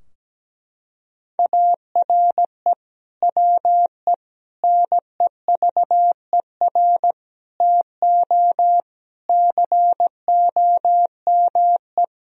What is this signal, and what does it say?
Morse code, heard as a keyed tone, about 17 wpm, 700 Hz. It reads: ARE WE NEVER TO COME
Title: Sentences from Top 1000 Words - Review Part 3 - 17wpm